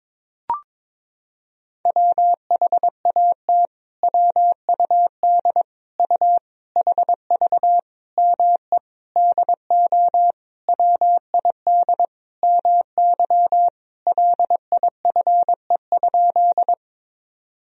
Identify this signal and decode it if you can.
Morse code, heard as a keyed tone, about 21 wpm, 700 Hz. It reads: WHAT WUD U HV ME DO WID MY LIFE?
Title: Sentences from Top 300 Words and Abbreviations 22wpm